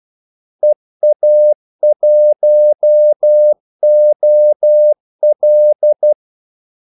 Morse code, heard as a keyed tone, about 12 wpm, 600 Hz. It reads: EA1OL
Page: 2 ekstra ordmellemrum mellem kaldesignalerne